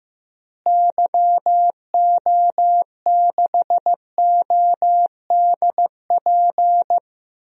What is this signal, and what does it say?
Morse code, heard as a keyed tone, about 15 wpm, 700 Hz. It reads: YO6ODP
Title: DXCC-100 Call Signs - 6 Characters - Repeated 2x - 15wpm